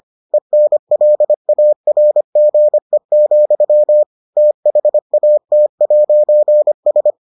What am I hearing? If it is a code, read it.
Morse code, heard as a keyed tone, about 25 wpm, 600 Hz. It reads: ENLARGE, THAT'S